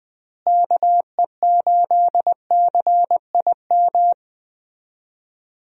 Morse code, heard as a keyed tone, about 20 wpm, 700 Hz. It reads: KE8CIM
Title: USA Call Signs - 6 Characters - Repeated 3x - 20wpm